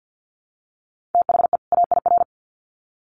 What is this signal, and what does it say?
Morse code, heard as a keyed tone, about 50 wpm, 700 Hz. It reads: THE AIR
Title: Sets of 2 Words 50wpm